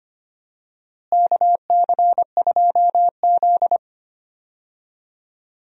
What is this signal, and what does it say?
Morse code, heard as a keyed tone, about 25 wpm, 700 Hz. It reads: KC2Z
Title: USA Call Signs - 4 Characters - Repeated 2x - 25wpm